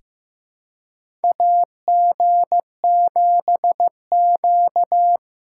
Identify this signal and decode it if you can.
Morse code, heard as a keyed tone, about 15 wpm, 700 Hz. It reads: AG7Q
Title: USA Call Signs - 4 Characters - 3x Character Spacing - 15wpm